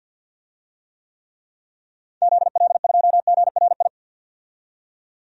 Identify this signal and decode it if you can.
Morse code, heard as a keyed tone, about 52 wpm, 700 Hz. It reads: ZL2GRI
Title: DXCC-100 Call Signs - 6 Characters - 50wpm